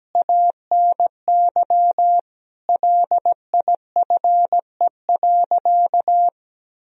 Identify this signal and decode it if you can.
Morse code, heard as a keyed tone, about 17 wpm, 700 Hz. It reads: ANY LIFE.
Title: Sentences from Top 300 Words and Abbreviations 17wpm